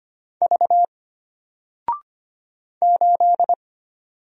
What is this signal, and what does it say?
Morse code, heard as a keyed tone, about 25 wpm, 700 Hz. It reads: V 8
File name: Single-Letter-Number-25wpm